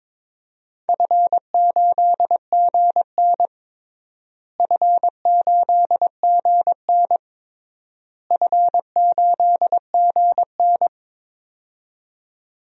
Morse code, heard as a keyed tone, about 22 wpm, 700 Hz. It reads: F8GN F8GN F8GN
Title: DXCC-100 Call Signs - 4 Characters - Repeated 3x - 22wpm